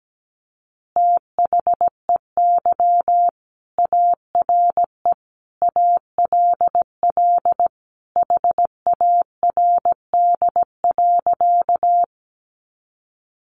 Morse code, heard as a keyed tone, about 17 wpm, 700 Hz. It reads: THEY ARE ALL HARD.
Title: Additional Easy Phrases 17wpm